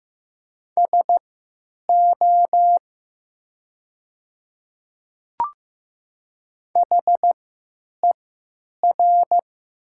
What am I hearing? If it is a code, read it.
Morse code, heard as a keyed tone, about 15 wpm, spaced as for 8 wpm, 700 Hz. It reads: SO HER
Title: 3x Character Spacing - Top 200 Words - 15wpm